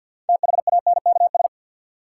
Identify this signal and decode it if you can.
Morse code, heard as a keyed tone, about 47 wpm, 700 Hz. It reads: THANKS
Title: Sentences from Top 600 Words - New Word Drill - 50wpm